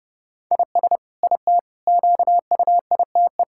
Morse code, heard as a keyed tone, about 30 wpm, 700 Hz. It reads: IS IT QUITE
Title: Sentences from Top 1000 Words - Review Part 6 - 30wpm